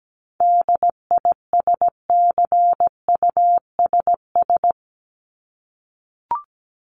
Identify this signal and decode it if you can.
Morse code, heard as a keyed tone, about 17 wpm, 700 Hz. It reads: DISCUSS
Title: Top 751-1000 Words 17wpm